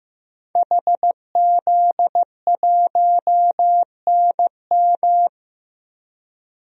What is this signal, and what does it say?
Morse code, heard as a keyed tone, about 15 wpm, 700 Hz. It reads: HZ1NM
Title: DXCC-100 Call Signs - 5 Characters - Repeated 2x - 15wpm